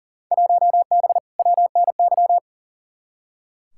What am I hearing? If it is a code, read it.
Morse code, heard as a keyed tone, about 40 wpm, 700 Hz. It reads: 1B WNY
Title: Field Day Exchanges - 40wpm